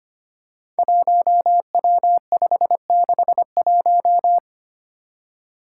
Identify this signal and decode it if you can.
Morse code, heard as a keyed tone, about 25 wpm, 700 Hz. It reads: 1W561
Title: Five Challenging Letter-Number Combinations 25wpm